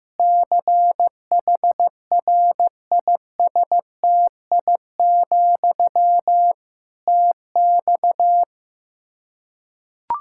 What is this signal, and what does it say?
Morse code, heard as a keyed tone, about 15 wpm, 700 Hz. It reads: CHRISTI, TX
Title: K1USN SST QSO Practice 15wpm